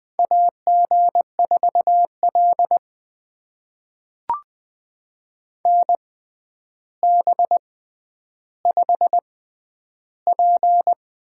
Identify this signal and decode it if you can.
Morse code, heard as a keyed tone, about 21 wpm, spaced as for 10 wpm, 700 Hz. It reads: AG4L NB5P
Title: USA Call Signs - 4 Characters - 6x Character Spacing - 20wpm